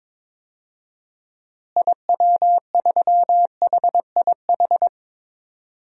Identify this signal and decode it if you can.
Morse code, heard as a keyed tone, about 21 wpm, 700 Hz. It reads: IW3HIH
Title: DXCC-100 Call Signs - 6 Characters - Repeated 2x - 22wpm